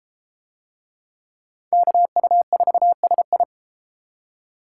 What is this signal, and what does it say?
Morse code, heard as a keyed tone, about 32 wpm, 700 Hz. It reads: KU4SI